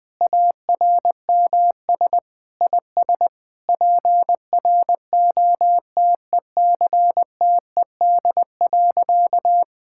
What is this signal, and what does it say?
Morse code, heard as a keyed tone, about 21 wpm, 700 Hz. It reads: ARMS IS PROTECTED.
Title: Sentences with 9-Word Phrases - 20wpm